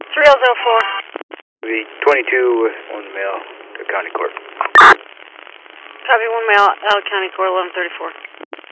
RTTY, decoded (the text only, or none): none